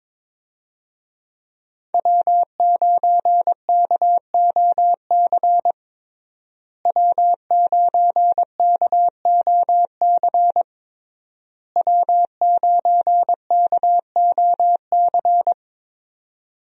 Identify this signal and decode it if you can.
Morse code, heard as a keyed tone, about 21 wpm, 700 Hz. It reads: W9KOC W9KOC W9KOC
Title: USA Call Signs - 5 Characters - Repeated 3x - 22wpm